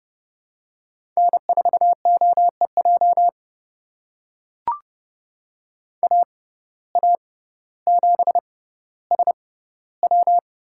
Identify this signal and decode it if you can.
Morse code, heard as a keyed tone, about 30 wpm, spaced as for 15 wpm, 700 Hz. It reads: N4OEJ AA7SW
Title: USA Call Signs - 5 Characters - 6x Character Spacing - 30wpm